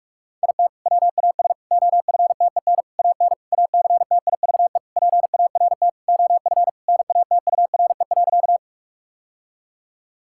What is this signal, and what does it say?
Morse code, heard as a keyed tone, about 46 wpm, 700 Hz. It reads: IT WAS OFTEN AN ACTIVE PART OF NATURE.
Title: Sentences from Top 1000 Words - Review Part 6 - 45wpm